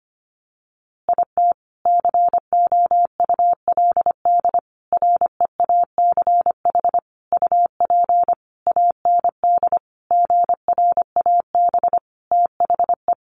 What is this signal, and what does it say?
Morse code, heard as a keyed tone, about 25 wpm, 700 Hz. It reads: IT COULD REACH UP AND GRAB THE
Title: A Sound Of Thunder 25wpm